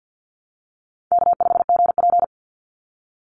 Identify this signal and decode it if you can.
Morse code, heard as a keyed tone, about 48 wpm, 700 Hz. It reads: K5GP
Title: Call Signs Repeated 3x - 50wpm